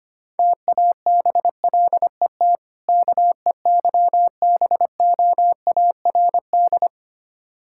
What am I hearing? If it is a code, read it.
Morse code, heard as a keyed tone, about 25 wpm, 700 Hz. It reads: TABLET KEYBOARD
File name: practice / 25wpm / Difficult Words with Hints - 25wpm